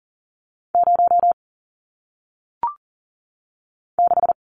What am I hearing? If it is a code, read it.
Morse code, heard as a keyed tone, about 39 wpm, 700 Hz. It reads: 0 6
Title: Numbers - 1-Digit 40wpm